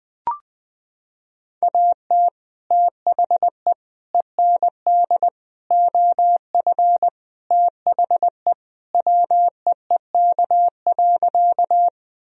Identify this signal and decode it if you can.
Morse code, heard as a keyed tone, about 21 wpm, 700 Hz. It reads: AT THE END OF THE WEEK.
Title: Sentences from Top 1000 Words - Review Part 2 - 20wpm